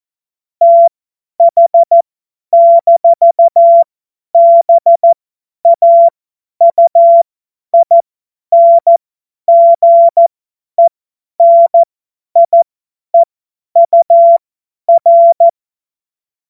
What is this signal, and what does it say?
Morse code, heard as a keyed tone, about 14 wpm, spaced as for 10 wpm, 680 Hz. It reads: TH-BAUINGENIEUR
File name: Morsekurs / Witze / Witz 31. Mai 2022 50 F70